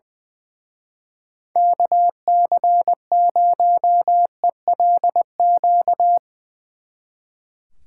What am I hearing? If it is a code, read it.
Morse code, heard as a keyed tone, about 22 wpm, 700 Hz. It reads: KC0ELQ
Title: Speed Racing - USA 2x3 Call Signs 20wpm